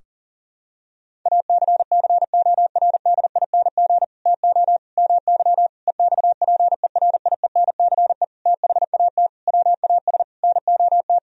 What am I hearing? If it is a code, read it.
Morse code, heard as a keyed tone, about 40 wpm, 700 Hz. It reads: ACCORDING TO MY EXPERIENCE THAT WAS NOT
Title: Sentences from Top 1000 Words - Review Part 4 - 40wpm